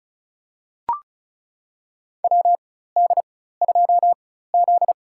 Morse code, heard as a keyed tone, about 34 wpm, spaced as for 16 wpm, 700 Hz. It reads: WD2Z
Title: USA Call Signs - 4 Characters - 4x Character Spacing - 35wpm